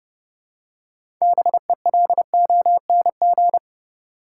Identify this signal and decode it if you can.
Morse code, heard as a keyed tone, about 30 wpm, 700 Hz. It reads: BELONG
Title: Six Letter Words - 30wpm